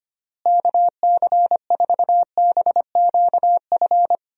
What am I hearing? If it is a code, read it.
Morse code, heard as a keyed tone, about 25 wpm, 700 Hz. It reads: KC4BQF
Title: USA Call Signs - 6 Characters - Repeated 3x - 25wpm